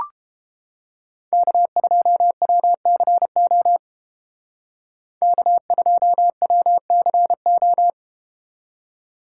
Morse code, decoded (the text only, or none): K2WCO K2WCO